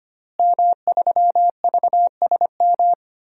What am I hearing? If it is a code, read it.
Morse code, heard as a keyed tone, about 25 wpm, 700 Hz. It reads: M3VSM
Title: DXCC-100 Call Signs - 5 Characters - Repeated 3x - 25wpm